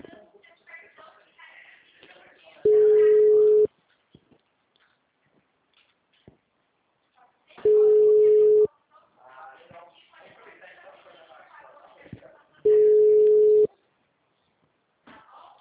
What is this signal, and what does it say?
Ringback tone (Europe, 425 Hz), with speech.
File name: out- -2005- - - .41121